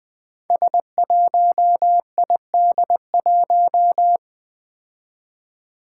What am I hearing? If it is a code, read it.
Morse code, heard as a keyed tone, about 20 wpm, 700 Hz. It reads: S1ID1
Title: Five Challenging Letter-Number Combinations 20wpm